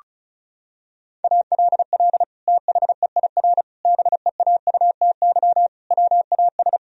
Morse code, heard as a keyed tone, about 36 wpm, 700 Hz. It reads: ALL THEIR BEAUTY WAS
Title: Sentences from Top 1000 Words - Review Part 2 - 35wpm